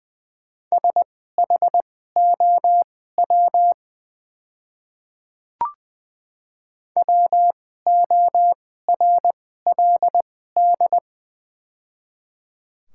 Morse code, heard as a keyed tone, about 20 wpm, spaced as for 14 wpm, 690 Hz. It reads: SHOW WORLD
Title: 2x Character Spacing - Top 200 Words - 20wpm